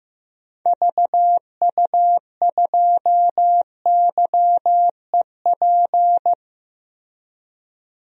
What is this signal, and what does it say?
Morse code, heard as a keyed tone, about 16 wpm, 700 Hz. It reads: VU2YEP
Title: DXCC-100 Call Signs - 6 Characters - Repeated 2x - 15wpm